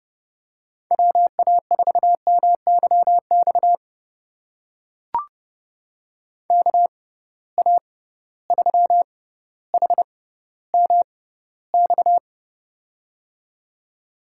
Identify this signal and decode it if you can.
Morse code, heard as a keyed tone, about 30 wpm, spaced as for 15 wpm, 700 Hz. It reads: WA4MYX KA3HMX
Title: USA Call Signs - 6 Characters - 6x Character Spacing - 30wpm